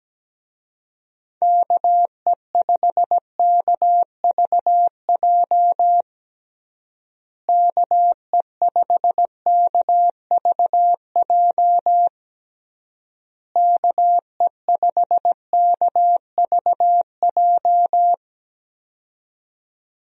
Morse code, heard as a keyed tone, about 17 wpm, 700 Hz. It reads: KE5KVJ KE5KVJ KE5KVJ
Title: USA Call Signs - 6 Characters - Repeated 3x - 17wpm